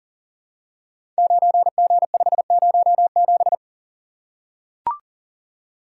Morse code, heard as a keyed tone, about 42 wpm, 700 Hz. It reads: 9G507
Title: Code Groups - 5 Letter:Numbers - 40wpm